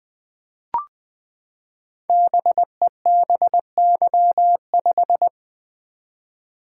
Morse code, heard as a keyed tone, about 20 wpm, 700 Hz. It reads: BEBY5
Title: Code Groups - 5 Letter:Numbers - 20wpm